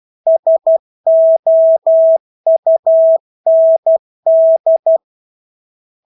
Morse code, heard as a keyed tone, about 12 wpm, 650 Hz. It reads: SOUND
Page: Engelske ord 12 wpm